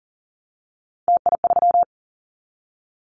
Morse code, heard as a keyed tone, about 38 wpm, 700 Hz. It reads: TI3
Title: Callsign Prefixes for Top 100 DXCC - 40wpm